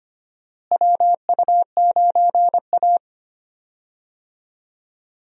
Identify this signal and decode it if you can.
Morse code, heard as a keyed tone, about 25 wpm, 700 Hz. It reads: WU9A
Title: USA Call Signs - 4 Characters - Repeated 2x - 25wpm